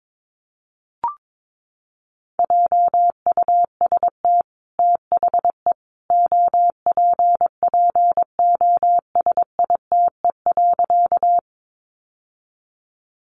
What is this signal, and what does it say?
Morse code, heard as a keyed tone, about 22 wpm, 700 Hz. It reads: JUST THE OPPOSITE.
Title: Sentences from Top 1300 Words 22wpm